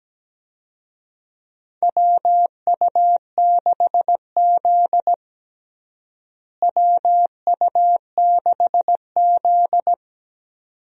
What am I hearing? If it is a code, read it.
Morse code, heard as a keyed tone, about 17 wpm, 700 Hz. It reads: WU6Z WU6Z
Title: USA Call Signs - 4 Characters - Repeated 2x - 17wpm